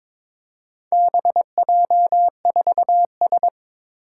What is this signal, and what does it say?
Morse code, heard as a keyed tone, about 22 wpm, 700 Hz. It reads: BJ4S
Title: Four Challenging Letter-Number Combinations 22wpm